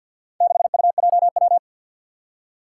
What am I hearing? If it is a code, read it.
Morse code, heard as a keyed tone, about 53 wpm, 700 Hz. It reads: 6UJW
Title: Four Challenging Letter-Number Combinations 50wpm